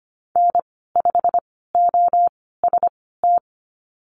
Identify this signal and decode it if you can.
Morse code, heard as a keyed tone, about 25 wpm, spaced as for 16 wpm, 700 Hz. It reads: N5OST
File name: Lesson 15g - Number 5 - Call Signs 25wpm